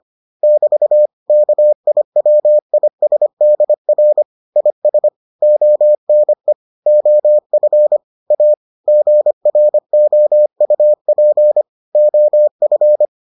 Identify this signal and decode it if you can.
Morse code, heard as a keyed tone, about 25 wpm, 600 Hz. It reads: <BT> KIWISDR IS ONE OF A GROUP OF